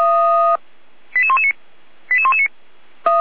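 Five-tone selective call (ZVEI), heard as quick runs of five tones, tones of about 70 ms. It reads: 88119 88119
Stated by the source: ZVEI five-tone selective call received on an SDR.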